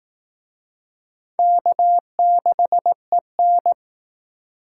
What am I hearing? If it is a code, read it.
Morse code, heard as a keyed tone, about 19 wpm, 700 Hz. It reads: K6EN